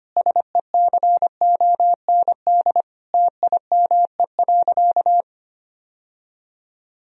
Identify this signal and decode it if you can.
Morse code, heard as a keyed tone, about 25 wpm, 700 Hz. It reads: SECOND TIME.
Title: Sentences from Top 1000 Words - Review Part 1 - 25wpm